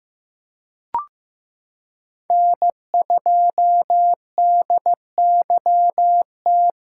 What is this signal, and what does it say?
Morse code, heard as a keyed tone, about 15 wpm, 700 Hz. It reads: N2DYT
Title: K1USN SST QSO Practice - 15wpm